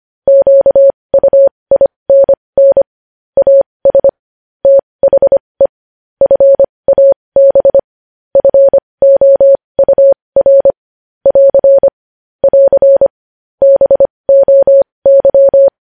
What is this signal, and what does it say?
Morse code, heard as a keyed tone, about 25 wpm, spaced as for 20 wpm, 550 Hz. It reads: QUINN AS THE FAB FOUR <AR> <AR> BOY
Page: News Headlines in Morse Code at 20 WPM